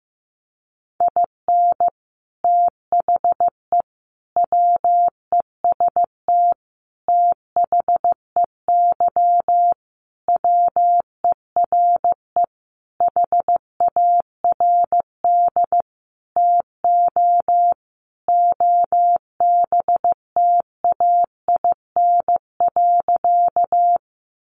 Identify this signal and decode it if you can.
Morse code, heard as a keyed tone, about 15 wpm, 700 Hz. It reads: IN THE WEST THEY WERE HARD TO OBTAIN.
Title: Sentences from Top 1300 Words 15wpm